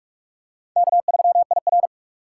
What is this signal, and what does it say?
Morse code, heard as a keyed tone, about 42 wpm, 700 Hz. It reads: K3IR